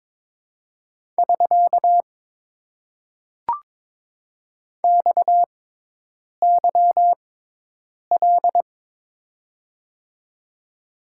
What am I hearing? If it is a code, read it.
Morse code, heard as a keyed tone, about 22 wpm, spaced as for 8 wpm, 700 Hz. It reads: <SK> XYL
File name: Top 90 QSO Elements - 6x Character Spacing - 22wpm